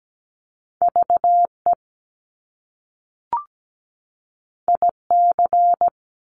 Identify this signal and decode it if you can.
Morse code, heard as a keyed tone, about 17 wpm, 700 Hz. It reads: VE IC
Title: Top Two Letter Combinations in Words 17wpm